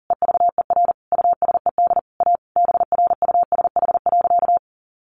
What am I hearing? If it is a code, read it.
Morse code, heard as a keyed tone, about 40 wpm, 700 Hz. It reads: EVER USED A BRUSH.
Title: Sentences from Top 1100 Words 40wpm